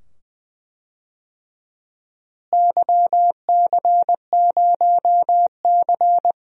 Morse code, heard as a keyed tone, about 21 wpm, 700 Hz. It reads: YC0C
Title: DXCC-100 Call Signs - 4 Characters - Repeated 2x - 20wpm